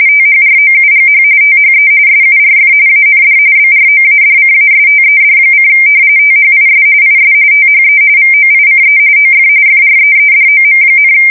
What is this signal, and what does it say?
RTTY, about 45.45 baud. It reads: THROUGH THE SURF AND BROADCASTED: THIS IS THE VOICE OF FREEDOM,
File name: fsk170 / battleshipnj